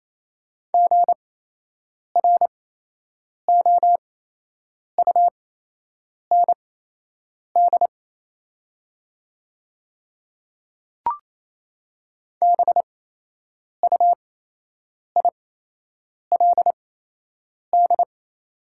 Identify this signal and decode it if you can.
Morse code, heard as a keyed tone, about 27 wpm, spaced as for 8 wpm, 690 Hz. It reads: GROUND BUILD
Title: 8x Character Spacing - Top 500 Words - 28wpm